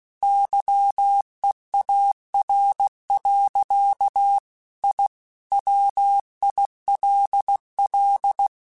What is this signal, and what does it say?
Morse code, heard as a keyed tone, about 16 wpm, 800 Hz. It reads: YEAR. I WILL